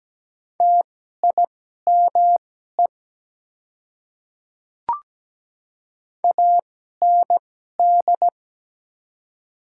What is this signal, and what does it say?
Morse code, heard as a keyed tone, about 17 wpm, spaced as for 12 wpm, 700 Hz. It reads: TIME AND
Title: 2x Character Spacing - Top 100 Words - 17wpm